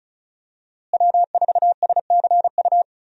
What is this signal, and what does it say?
Morse code, heard as a keyed tone, about 36 wpm, 700 Hz. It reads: W4SCU
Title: USA Call Signs - 5 Characters - Repeated 2x - 35wpm